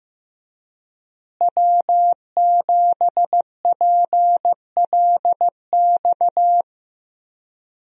Morse code, heard as a keyed tone, about 16 wpm, 700 Hz. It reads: W7PLX
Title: USA Call Signs - 5 Characters - Repeated 3x - 15wpm